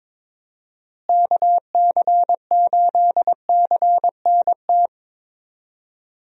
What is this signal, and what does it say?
Morse code, heard as a keyed tone, about 22 wpm, 700 Hz. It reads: KC8CNT
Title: USA Call Signs - 6 Characters - Repeated 2x - 22wpm